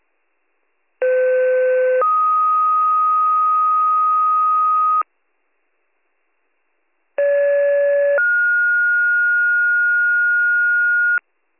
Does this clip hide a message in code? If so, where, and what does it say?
Yes: two-tone page from 1.0 to 11.2 s, heard as a one-second tone, then a three-second tone, about 510 then 1250 Hz, about 570 then 1430 Hz.